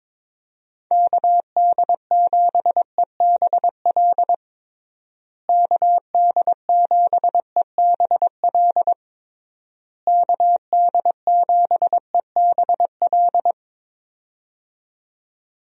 Morse code, heard as a keyed tone, about 22 wpm, 700 Hz. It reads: KD7EBL KD7EBL KD7EBL
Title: USA Call Signs - 6 Characters - Repeated 3x - 22wpm